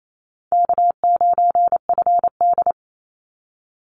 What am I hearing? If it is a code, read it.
Morse code, heard as a keyed tone, about 28 wpm, 700 Hz. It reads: K9FD
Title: Easy Call Signs - 28wpm